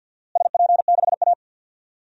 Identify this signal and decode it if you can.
Morse code, heard as a keyed tone, about 47 wpm, 700 Hz. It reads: SP6A
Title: DXCC-100 Call Signs - 4 Characters - Repeated 2x - 50wpm